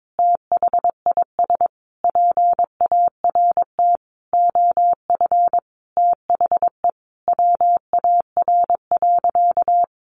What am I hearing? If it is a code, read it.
Morse code, heard as a keyed tone, about 22 wpm, 700 Hz. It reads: THIS PART OF THE WAR.
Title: Sentences from Top 1500 Words 22wpm